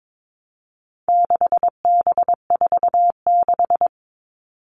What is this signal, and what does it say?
Morse code, heard as a keyed tone, about 22 wpm, 700 Hz. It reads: 6B46
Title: All Cut Numbers - 4-Digit 22wpm